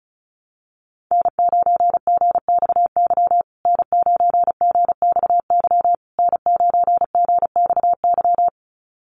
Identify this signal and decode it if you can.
Morse code, heard as a keyed tone, about 34 wpm, 700 Hz. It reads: N9GXY N9GXY N9GXY
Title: USA Call Signs Repeated 3x 35wpm